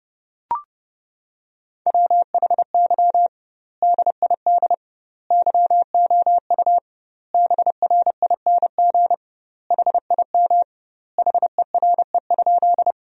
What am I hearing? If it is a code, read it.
Morse code, heard as a keyed tone, about 32 wpm, 700 Hz. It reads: WHY DID YOU BRING HIM HERE?
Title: 2x Word-Spacing - Sentences from Top 500 Words - 30wpm